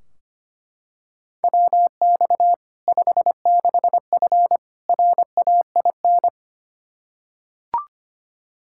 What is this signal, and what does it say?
Morse code, heard as a keyed tone, about 25 wpm, 700 Hz. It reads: WX 56F RAIN
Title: Practice QSO - Exchange 2 - Info - 25wpm